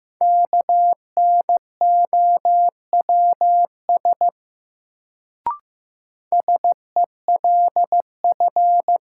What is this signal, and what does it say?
Morse code, heard as a keyed tone, about 15 wpm, 700 Hz. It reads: KNOWS SELF
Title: Sentences from Top 1000 Words - Review Part 1 - New Word Drill - 15wpm